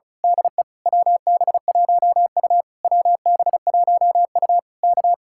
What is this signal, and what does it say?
Morse code, heard as a keyed tone, about 33 wpm, 700 Hz. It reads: DE WB1U WB1U K
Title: Speed-Racing Common QSO Phrases - 35wpm